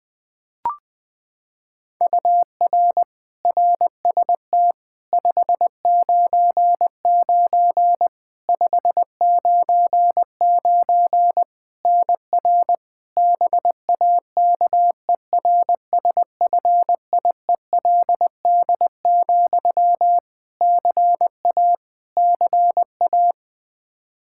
Morse code, heard as a keyed tone, about 20 wpm, 700 Hz. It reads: UR RST 599 599 NR BAKERSFIELD, CA CA
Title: Practice QSO - Exchange 1 - Beginning - 20wpm